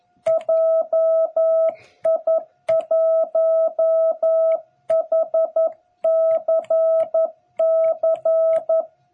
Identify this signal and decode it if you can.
Morse code, heard as a keyed tone, about 11 wpm, 660 Hz. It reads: JI1HCC
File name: jh8cbh / 2021cw